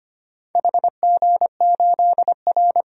Morse code, heard as a keyed tone, about 25 wpm, 700 Hz. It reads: HG8R
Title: DXCC-100 Call Signs - 4 Characters - Repeated 3x - 25wpm